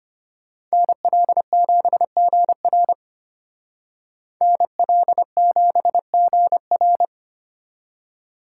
Morse code, heard as a keyed tone, about 31 wpm, 700 Hz. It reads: NL7GR NL7GR
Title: Speed Racing - USA 2x2 Call Signs 25wpm